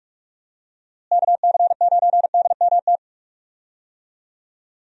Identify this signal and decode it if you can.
Morse code, heard as a keyed tone, about 43 wpm, 700 Hz. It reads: KC9DMT
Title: USA Call Signs - 6 Characters - Repeated 2x - 45wpm